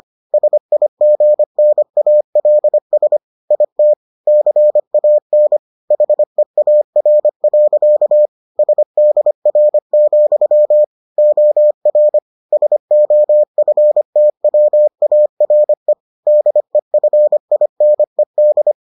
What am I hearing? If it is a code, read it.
Morse code, heard as a keyed tone, about 25 wpm, 600 Hz. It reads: SIGNALS IT CAN HEAR. SDR, OR SOFTWARE DEFINED